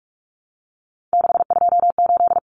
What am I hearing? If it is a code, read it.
Morse code, heard as a keyed tone, about 44 wpm, 700 Hz. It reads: 628
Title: Numbers - 3-Digit 45wpm